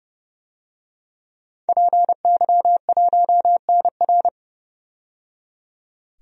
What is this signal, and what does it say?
Morse code, heard as a keyed tone, about 29 wpm, 700 Hz. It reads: PY1NR
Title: DXCC-100 Call Signs - 5 Characters - 30wpm